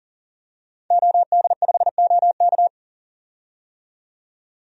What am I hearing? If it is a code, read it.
Morse code, heard as a keyed tone, about 40 wpm, 700 Hz. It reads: OD5OK
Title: DXCC-100 Call Signs - 5 Characters - 40wpm